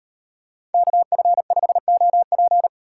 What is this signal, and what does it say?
Morse code, heard as a keyed tone, about 35 wpm, 700 Hz. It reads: KF5OP